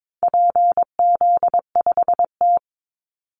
Morse code, heard as a keyed tone, about 22 wpm, 700 Hz. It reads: PZ5T
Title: Call Signs Repeated 2x - 22wpm